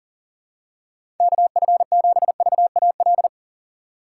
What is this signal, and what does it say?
Morse code, heard as a keyed tone, about 37 wpm, 700 Hz. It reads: KF7VAL
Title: USA Call Signs - 6 Characters - 3x Character Spacing - 40wpm